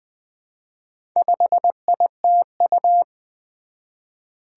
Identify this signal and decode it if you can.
Morse code, heard as a keyed tone, about 20 wpm, 700 Hz. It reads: 5ITU
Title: Four Challenging Letter-Number Combinations 20wpm